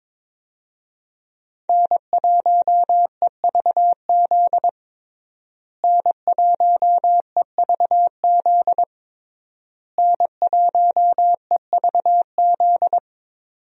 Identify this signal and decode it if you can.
Morse code, heard as a keyed tone, about 21 wpm, 700 Hz. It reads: N1EVZ N1EVZ N1EVZ
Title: USA Call Signs - 5 Characters - Repeated 3x - 22wpm